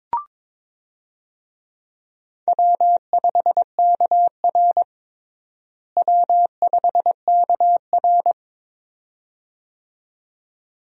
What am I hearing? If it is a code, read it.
Morse code, heard as a keyed tone, about 22 wpm, 700 Hz. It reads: W5KR W5KR
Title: USA Call Signs - 4 Characters - Repeated 2x - 22wpm